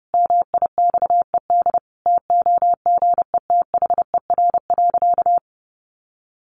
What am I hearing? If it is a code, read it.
Morse code, heard as a keyed tone, about 30 wpm, 700 Hz. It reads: MIXED TOGETHER.
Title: Sentences from Top 1500 Words 30wpm